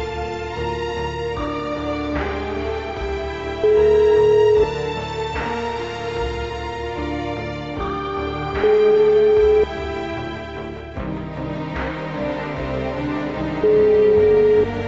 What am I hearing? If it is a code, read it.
Ringback tone (Europe, 425 Hz), with noise.